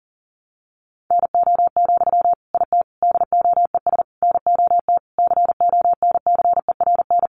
Morse code, heard as a keyed tone, about 40 wpm, 700 Hz. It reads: NO, IT DOES NOT CONCERN
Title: Sentences from Top 1000 Words 40wpm